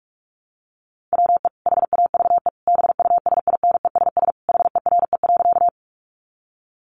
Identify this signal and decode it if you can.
Morse code, heard as a keyed tone, about 44 wpm, 700 Hz. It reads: WE HAVE BUSINESS HERE.
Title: Sentences from Top 600 Words 45wpm